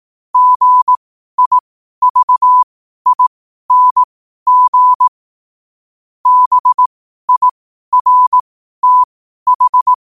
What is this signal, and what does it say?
Morse code, heard as a keyed tone, about 18 wpm, spaced as for 12 wpm, 1000 Hz. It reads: GIVING BIRTH
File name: qotdfiles / 12wpm